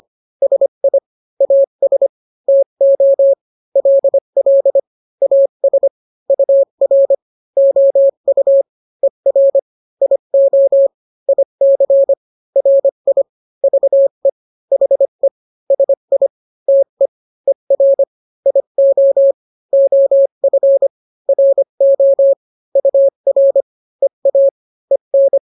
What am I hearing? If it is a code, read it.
Morse code, heard as a keyed tone, about 25 wpm, 550 Hz. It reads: SI AS TO LL AS UR OU ER IO IC RI VE HE SI TE ER IO OF RO UR EA EN